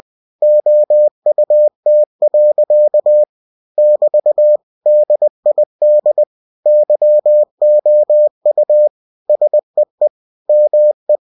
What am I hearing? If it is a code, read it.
Morse code, heard as a keyed tone, about 20 wpm, 600 Hz. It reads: OUT. <BT> DID YOU SEE ME
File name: cw-books / winnie-the-pooh / winnie-20wpm-page-0087